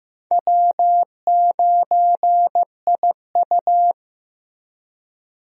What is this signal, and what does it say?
Morse code, heard as a keyed tone, about 15 wpm, 700 Hz. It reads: W9IU
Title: USA Call Signs - 4 Characters - Repeated 2x - 15wpm